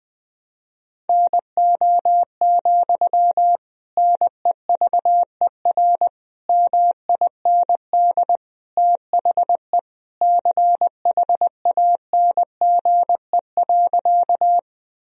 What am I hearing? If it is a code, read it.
Morse code, heard as a keyed tone, about 21 wpm, 700 Hz. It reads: NO, NEVER MIND THE CHANGE.
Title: Sentences from Top 1000 Words - Review Part 1 - 20wpm